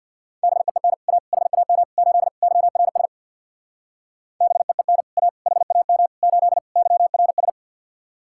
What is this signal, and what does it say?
Morse code, heard as a keyed tone, about 64 wpm, 700 Hz. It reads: BEEN A HAM 8 YRS BEEN A HAM 8 YRS